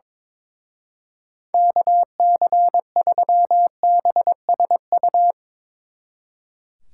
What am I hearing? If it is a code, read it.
Morse code, heard as a keyed tone, about 22 wpm, 700 Hz. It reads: KC3BSU
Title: Speed Racing - USA 2x3 Call Signs 22wpm